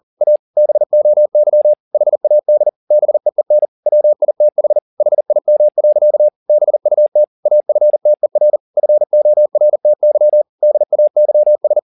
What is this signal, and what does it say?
Morse code, heard as a keyed tone, about 40 wpm, 600 Hz. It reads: A BOY HAD BEEN WITH HIM. BUT AFTER FORTY DAYS
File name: the old man and the sea 1 40wpm